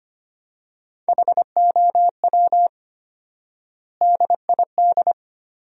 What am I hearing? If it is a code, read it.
Morse code, heard as a keyed tone, about 25 wpm, 700 Hz. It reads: HOW DID
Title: 4x Word-Spacing - Sentences from Top 300 Words - 25wpm